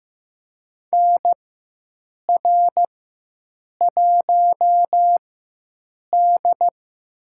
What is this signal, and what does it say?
Morse code, heard as a keyed tone, about 15 wpm, spaced as for 7 wpm, 700 Hz. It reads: NR1D
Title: USA Call Signs - 4 Characters - 4x Character Spacing - 15wpm